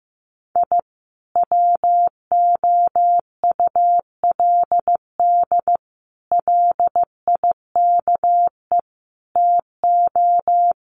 Morse code, heard as a keyed tone, about 15 wpm, 700 Hz. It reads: I WOULD LIKE TO
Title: Sets of 5 Words Encore2 15wpm